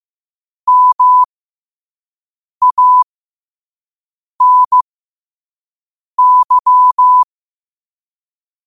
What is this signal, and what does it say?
Morse code, heard as a keyed tone, about 15 wpm, spaced as for 5 wpm, 1000 Hz. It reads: MANY
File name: qotdfiles / 5wpm